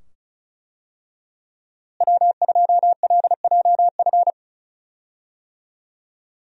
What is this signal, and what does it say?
Morse code, heard as a keyed tone, about 36 wpm, 700 Hz. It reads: W2LJF
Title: USA Call Signs - 5 Characters - Repeated 2x - 35wpm